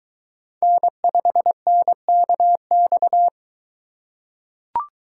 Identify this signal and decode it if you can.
Morse code, heard as a keyed tone, about 23 wpm, 700 Hz. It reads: N5NKX